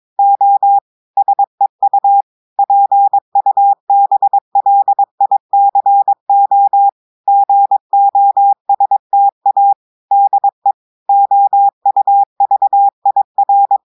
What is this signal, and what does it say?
Morse code, heard as a keyed tone, about 22 wpm, 800 Hz. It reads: O SEU PUBLICO GOSTA DE OUVIR
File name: audios / portugais / portugais texte en clair 2 vitesse 22 mots minutes